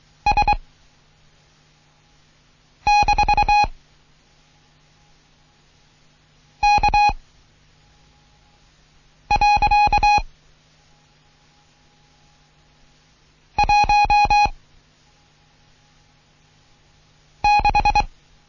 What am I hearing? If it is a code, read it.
Morse code, heard as a keyed tone, about 23 wpm, 830 Hz. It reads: S - K . 1 6
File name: Morsen / Kurs A 16